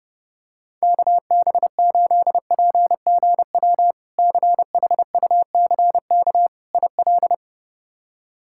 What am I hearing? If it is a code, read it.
Morse code, heard as a keyed tone, about 32 wpm, 700 Hz. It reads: KB8PGW CHUCK IL
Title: K1USN SST QSO Practice - 30wpm